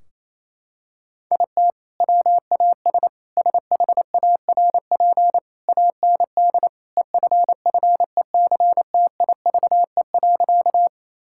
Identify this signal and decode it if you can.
Morse code, heard as a keyed tone, about 28 wpm, 700 Hz. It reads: IT WAS SHARP AND EFFECTIVE.
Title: Sentences from Top 1600 Words 28wpm